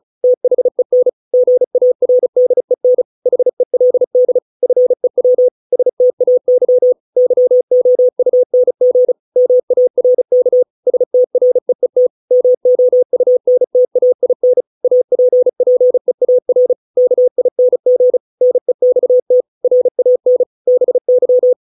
Morse code, heard as a keyed tone, about 34 wpm, 500 Hz. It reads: THEN GARDEN HELD FEW STAY YOUNG MARK STREET MOUNTAIN APPEAR KING NEXT RAN BY